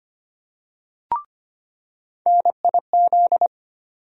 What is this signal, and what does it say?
Morse code, heard as a keyed tone, about 25 wpm, 700 Hz. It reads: NIZ
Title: USA Call Sign Postfixes - 3 Letters 25wpm